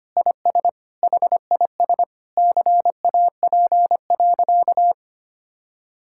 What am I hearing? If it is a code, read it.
Morse code, heard as a keyed tone, about 25 wpm, 700 Hz. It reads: IS HIS CAP.
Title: Sentences from Top 1600 Words 25wpm